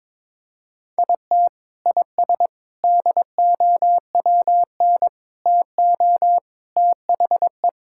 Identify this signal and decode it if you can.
Morse code, heard as a keyed tone, about 22 wpm, 700 Hz. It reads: IT IS DOWN TO THE
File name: practice / 22wpm / Sentences with 3-Word Phrases - 22wpm